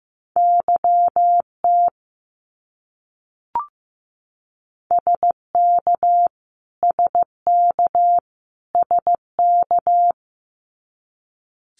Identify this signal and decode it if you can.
Morse code, heard as a keyed tone, about 14 wpm, 700 Hz. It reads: YT SK SK SK
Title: Canadian Provinces and Territories - Repeated 3x - 15wpm